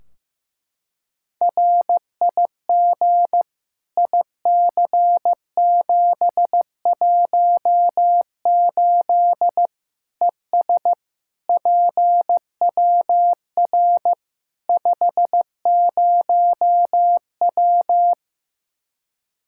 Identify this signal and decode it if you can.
Morse code, heard as a keyed tone, about 15 wpm, 700 Hz. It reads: RIG IC718 ES PWR 50W
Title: Practice QSO - Exchange 2 - Info - 15wpm